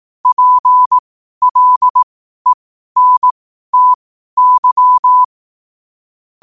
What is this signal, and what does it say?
Morse code, heard as a keyed tone, about 18 wpm, spaced as for 12 wpm, 1000 Hz. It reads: PLENTY
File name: qotdfiles / 12wpm